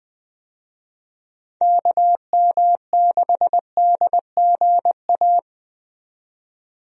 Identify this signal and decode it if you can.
Morse code, heard as a keyed tone, about 20 wpm, 700 Hz. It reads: KM6DGA
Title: USA Call Signs - 6 Characters - 20wpm